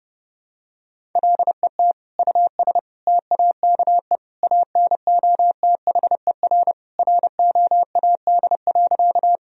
Morse code, heard as a keyed tone, about 30 wpm, 700 Hz. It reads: LET US TAKE ANOTHER ROAD.
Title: Sentences from Top 1000 Words - Review Part 2 - 30wpm